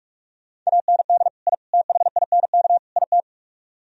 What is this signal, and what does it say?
Morse code, heard as a keyed tone, about 44 wpm, 700 Hz. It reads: AND I THINK IT